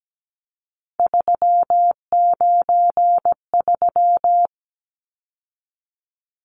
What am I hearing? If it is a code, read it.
Morse code, heard as a keyed tone, about 17 wpm, 700 Hz. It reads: 393
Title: Numbers - 3-Digit 17wpm